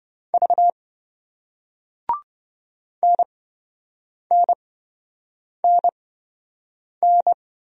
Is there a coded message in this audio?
Yes: Morse code from 0.3 to 7.3 s, heard as a keyed tone, about 26 wpm, spaced as for 7 wpm, 700 Hz. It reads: V NNNN